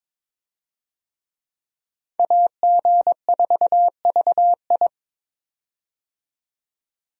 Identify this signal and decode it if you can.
Morse code, heard as a keyed tone, about 22 wpm, 700 Hz. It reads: AG4VI
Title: USA Call Signs - 5 Characters - 22wpm